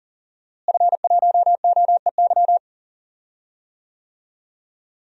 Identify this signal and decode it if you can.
Morse code, heard as a keyed tone, about 40 wpm, 700 Hz. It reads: F1OEY
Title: DXCC-100 Call Signs - 5 Characters - Repeated 3x - 40wpm